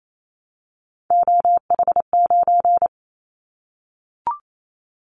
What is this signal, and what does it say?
Morse code, heard as a keyed tone, about 28 wpm, 700 Hz. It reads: OH9
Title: Callsign Prefixes for Top 50 DXCC - 28wpm